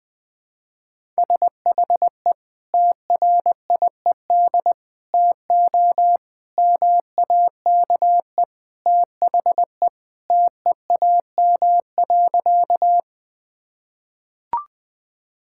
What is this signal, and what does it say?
Morse code, heard as a keyed tone, about 20 wpm, 700 Hz. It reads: SHE TRIED TO MAKE THE TEAM.
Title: Sentences with 3-Word Phrases - 20wpm